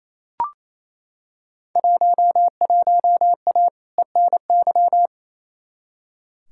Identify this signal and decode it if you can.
Morse code, heard as a keyed tone, about 28 wpm, 700 Hz. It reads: 11A ENY
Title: Field Day Exchanges Encore - 28wpm